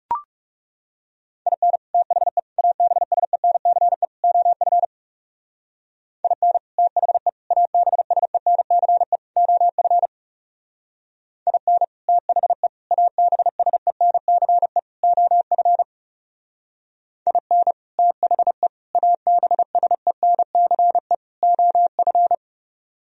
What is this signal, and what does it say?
Morse code, heard as a keyed tone, about 36 wpm, 700 Hz. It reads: IN THE ABSENCE OF IN THE ABSENCE OF IN THE ABSENCE OF IN THE ABSENCE OF
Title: Speed-Racing - Sets of 4 Words 30wpm